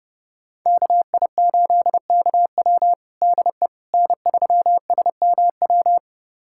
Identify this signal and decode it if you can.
Morse code, heard as a keyed tone, about 32 wpm, 700 Hz. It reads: KI8KW DE N3SMW
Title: USA Callsign DE Callsign - 5 Characters - 30wpm